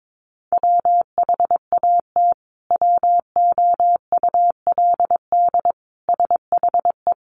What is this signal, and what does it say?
Morse code, heard as a keyed tone, about 22 wpm, 700 Hz. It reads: WHAT WOULD SHE
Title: Sentences from Top 100 Words 22wpm